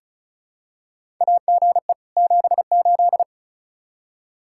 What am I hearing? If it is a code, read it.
Morse code, heard as a keyed tone, about 36 wpm, 700 Hz. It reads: AGE 78
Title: Practice QSO - Exchange 3 - More Info - 35wpm